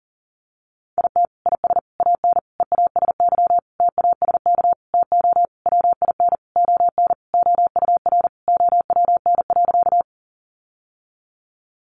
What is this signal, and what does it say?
Morse code, heard as a keyed tone, about 40 wpm, 700 Hz. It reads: IT IS AN EASY TASK TO WIN ON OUR OWN.
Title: Sentences from Top 1500 Words 40wpm